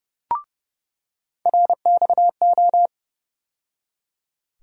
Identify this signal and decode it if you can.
Morse code, heard as a keyed tone, about 30 wpm, 700 Hz. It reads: RXO
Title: USA Call Sign Postfixes - 3 Letters 30wpm